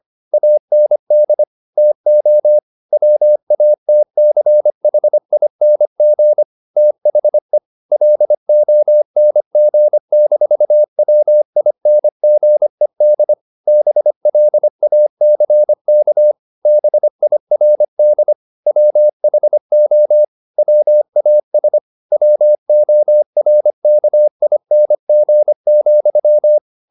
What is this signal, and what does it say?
Morse code, heard as a keyed tone, about 25 wpm, 600 Hz. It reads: AND TO WATCHING THE LONG-WINGED BLACK BIRD WHO WAS WORKING,
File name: the old man and the sea part 1 25wpm